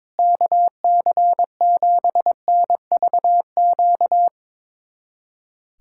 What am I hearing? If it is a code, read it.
Morse code, heard as a keyed tone, about 22 wpm, 700 Hz. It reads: KC7NVQ
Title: USA Callsign DE Callsign 22wpm